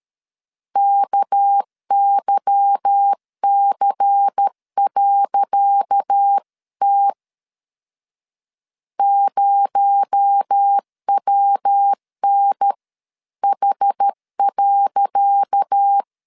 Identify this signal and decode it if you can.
Morse code, heard as a keyed tone, about 13 wpm, 800 Hz. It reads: KYC.T 0WNH.
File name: LEZIONI / 10 SIGNS / 10C DOT 12wpm 2451